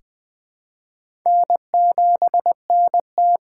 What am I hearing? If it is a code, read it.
Morse code, heard as a keyed tone, about 21 wpm, 700 Hz. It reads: N7NT
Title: K1USN SST QSO Practice - 20wpm